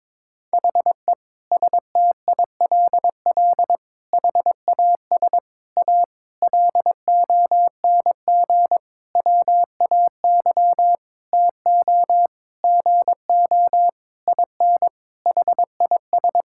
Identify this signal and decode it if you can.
Morse code, heard as a keyed tone, about 22 wpm, 700 Hz. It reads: HE STILL HAS A LONG WAY TO GO IN HIS
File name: Sentences with 7-Word Phrases - 22wpm